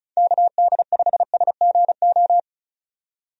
Morse code, heard as a keyed tone, about 34 wpm, 700 Hz. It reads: KD5SGO
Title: USA Call Signs - 6 Characters - 35wpm